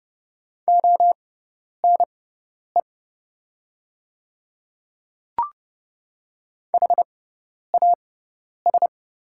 Morse code, heard as a keyed tone, about 30 wpm, spaced as for 10 wpm, 700 Hz. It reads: ONE HAS
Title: 6x Character Spacing - Top 100 Words - 30wpm